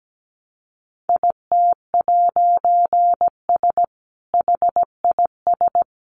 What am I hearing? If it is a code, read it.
Morse code, heard as a keyed tone, about 17 wpm, 700 Hz. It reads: IT'S HIS
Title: A Sound Of Thunder 17wpm